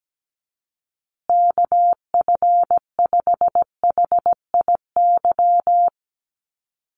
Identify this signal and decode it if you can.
Morse code, heard as a keyed tone, about 17 wpm, 700 Hz. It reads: KF5HIY
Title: USA Call Signs Repeated 3x 17wpm